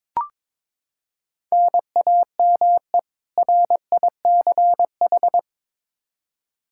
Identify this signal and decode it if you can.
Morse code, heard as a keyed tone, about 22 wpm, 700 Hz. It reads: NAME RICH
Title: K1USN SST QSO Practice 22wpm